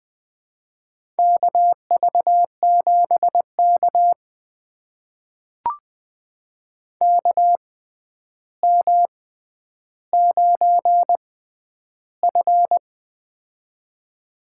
Morse code, heard as a keyed tone, about 20 wpm, spaced as for 10 wpm, 700 Hz. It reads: KV7K KM9F
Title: USA Call Signs - 4 Characters - 6x Character Spacing - 20wpm